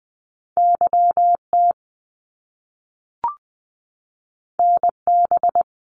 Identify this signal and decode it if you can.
Morse code, heard as a keyed tone, about 19 wpm, 700 Hz. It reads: YT NB
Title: Canadian Provinces and Territories 20wpm